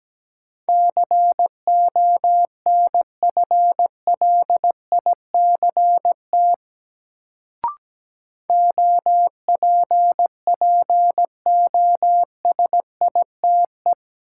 Morse code, heard as a keyed tone, about 17 wpm, 700 Hz. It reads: CONFLICT OPPOSITE
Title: Sentences from Top 1300 Words - New Word Drill - 17wpm